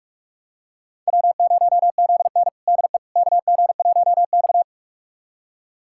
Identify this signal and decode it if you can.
Morse code, heard as a keyed tone, about 46 wpm, 700 Hz. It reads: W0ZN DE KG1X
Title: USA Callsign DE Callsign - 4 Characters - 45wpm